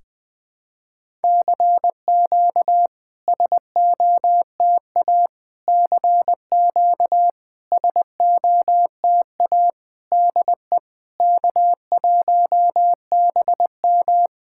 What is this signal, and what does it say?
Morse code, heard as a keyed tone, about 20 wpm, 700 Hz. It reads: CQ SOTA CQ SOTA DE K1BM
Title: SOTA QSO Practice 20wpm